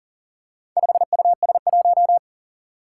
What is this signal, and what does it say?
Morse code, heard as a keyed tone, about 41 wpm, 700 Hz. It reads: 5US1
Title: Four Challenging Letter-Number Combinations 40wpm